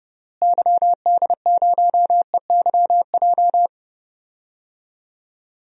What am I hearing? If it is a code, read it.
Morse code, heard as a keyed tone, about 30 wpm, 700 Hz. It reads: YD0EYJ
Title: DXCC-100 Call Signs - 6 Characters - Repeated 2x - 30wpm